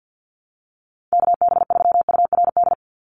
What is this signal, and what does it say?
Morse code, heard as a keyed tone, about 50 wpm, 700 Hz. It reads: KB3URD
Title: USA Call Signs 50wpm